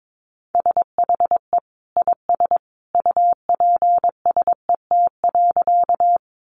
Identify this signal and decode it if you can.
Morse code, heard as a keyed tone, about 22 wpm, 700 Hz. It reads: SHE IS UPSET.
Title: Sentences from Top 800 Words 22wpm